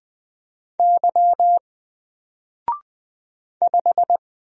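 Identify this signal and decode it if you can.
Morse code, heard as a keyed tone, about 20 wpm, 700 Hz. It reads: Y 5
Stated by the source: Single letter and number Morse drill at 20 wpm, from N4LCD.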